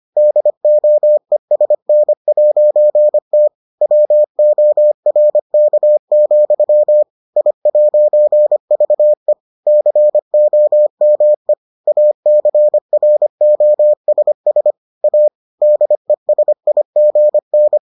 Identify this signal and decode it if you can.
Morse code, heard as a keyed tone, about 25 wpm, 600 Hz. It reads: DOESN'T WORK, I'VE COME ACROSS A DESIGN